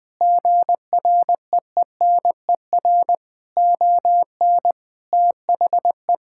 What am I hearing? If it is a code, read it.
Morse code, heard as a keyed tone, about 20 wpm, 700 Hz. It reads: GREENER ON THE
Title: Sentences with 4-Word Phrases - Spoken Beginning - 20wpm